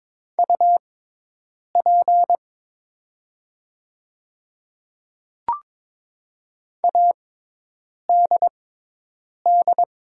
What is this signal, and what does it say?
Morse code, heard as a keyed tone, about 22 wpm, spaced as for 8 wpm, 690 Hz. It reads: UP ADD
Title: 6x Character Spacing - Top 300 Words - 22wpm